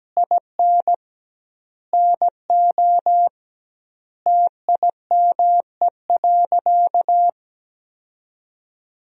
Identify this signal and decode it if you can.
Morse code, heard as a keyed tone, about 17 wpm, 700 Hz. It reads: IN NO TIME.
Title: 2x Word-Spacing - Sentences from Top 400 Words - 17wpm